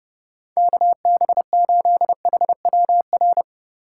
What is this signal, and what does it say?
Morse code, heard as a keyed tone, about 30 wpm, 700 Hz. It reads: KB8HWR
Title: USA Call Signs - 8x Character Spacing - 30wpm